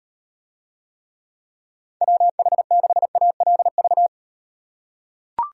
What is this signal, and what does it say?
Morse code, heard as a keyed tone, about 38 wpm, 700 Hz. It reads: WH6ALV